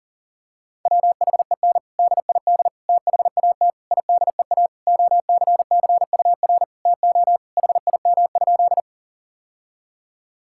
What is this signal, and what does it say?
Morse code, heard as a keyed tone, about 40 wpm, 700 Hz. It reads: WHEN DID THAT IDEA OCCUR TO HIM?
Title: Sentences from Top 1000 Words - Review Part 6 - 40wpm